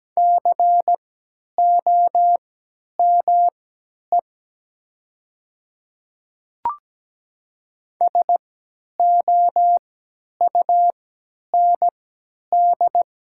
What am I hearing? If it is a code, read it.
Morse code, heard as a keyed tone, about 17 wpm, spaced as for 10 wpm, 700 Hz. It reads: COME SOUND
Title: 3x Character Spacing - Top 100 Words - 17wpm